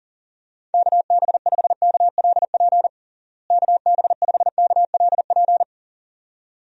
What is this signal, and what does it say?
Morse code, heard as a keyed tone, about 40 wpm, 700 Hz. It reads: KB5KLP KB5KLP
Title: USA Call Signs - 6 Characters - Repeated 2x - 40wpm